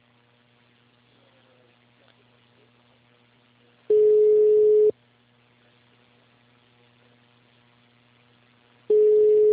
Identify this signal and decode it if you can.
Ringback tone (Europe, 425 Hz).
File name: out- -2041- - - .5916